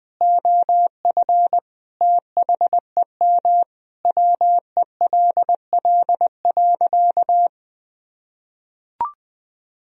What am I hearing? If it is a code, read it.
Morse code, heard as a keyed tone, about 20 wpm, 700 Hz. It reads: OF THEM WELL.
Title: Sentences from Top 1000 Words - Review Part 2 - 20wpm